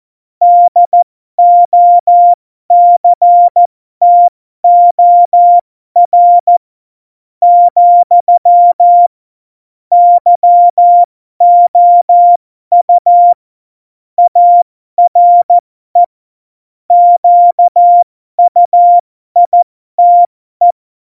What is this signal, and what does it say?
Morse code, heard as a keyed tone, about 14 wpm, spaced as for 12 wpm, 700 Hz. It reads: DOCTOR, YOU ARE QUITE
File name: Morsekurs / Witze